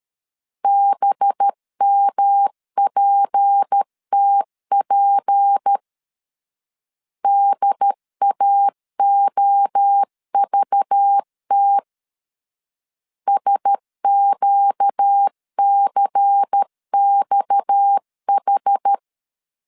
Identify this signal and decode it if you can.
Morse code, heard as a keyed tone, about 13 wpm, 800 Hz. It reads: BMPTP DAOVT SQCXH
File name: CW / DET / 1 6 1500C S C 12wpm 3533